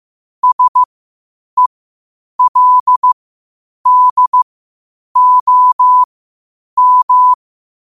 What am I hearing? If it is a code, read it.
Morse code, heard as a keyed tone, about 15 wpm, spaced as for 8 wpm, 1000 Hz. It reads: SELDOM
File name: qotdfiles / 8wpm